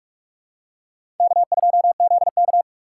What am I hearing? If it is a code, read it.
Morse code, heard as a keyed tone, about 42 wpm, 700 Hz. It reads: K2ZK
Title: USA Call Signs - 4 Characters - 6x Character Spacing - 45wpm